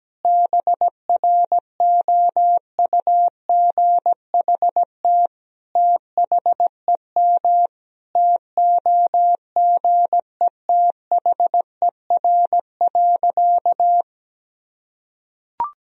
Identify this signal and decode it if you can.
Morse code, heard as a keyed tone, about 17 wpm, 700 Hz. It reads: BROUGHT THEM TOGETHER.
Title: Sentences from Top 1000 Words - Review Part 3 - 17wpm